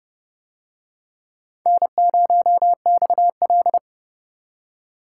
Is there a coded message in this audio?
Yes: Morse code from 1.7 to 3.8 s, heard as a keyed tone, about 30 wpm, 700 Hz. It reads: N0XL